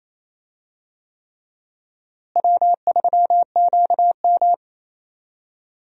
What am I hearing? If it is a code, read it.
Morse code, heard as a keyed tone, about 28 wpm, 700 Hz. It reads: W3QM
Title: USA Call Signs - 4 Characters - Repeated 3x - 28wpm